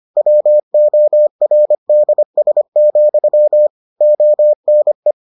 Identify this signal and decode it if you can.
Morse code, heard as a keyed tone, about 25 wpm, 600 Hz. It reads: WORDS, ONE